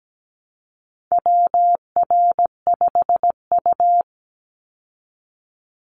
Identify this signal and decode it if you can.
Morse code, heard as a keyed tone, about 17 wpm, 700 Hz. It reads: WR5U
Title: Call Signs Repeated 2x - 17wpm